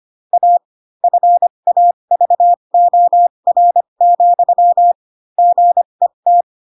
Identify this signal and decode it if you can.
Morse code, heard as a keyed tone, about 25 wpm, spaced as for 22 wpm, 700 Hz. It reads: A FAVOR, GET
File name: Pinocchio / Pinocchio ch4 18 25 ik1hge